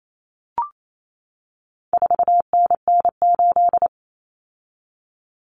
Morse code, heard as a keyed tone, about 28 wpm, 700 Hz. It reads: 4NN8
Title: Common Cut Numbers - 4-Digit 28wpm